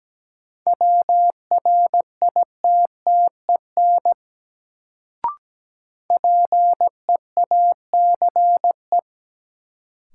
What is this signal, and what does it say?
Morse code, heard as a keyed tone, about 17 wpm, 700 Hz. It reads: WRITTEN PEACE
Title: Sentences from Top 1500 Words - New Word Drill Alt - 17wpm